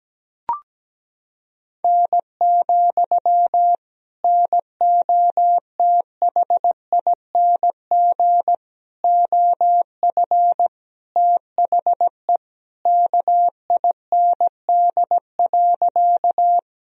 Morse code, heard as a keyed tone, about 17 wpm, 700 Hz. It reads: N, NOTHING OF THE KIND.
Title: Sentences from Top 500 Words and Abbreviations 17wpm